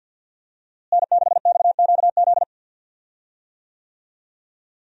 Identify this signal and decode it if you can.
Morse code, heard as a keyed tone, about 50 wpm, 700 Hz. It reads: N6XQZ
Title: USA Call Signs - 5 Characters - Repeated 2x - 50wpm